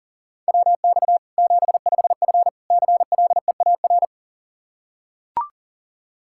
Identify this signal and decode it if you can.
Morse code, heard as a keyed tone, about 40 wpm, 700 Hz. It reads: WX 75F CLEAR
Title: Practice QSO - Exchange 2 - Info - 40wpm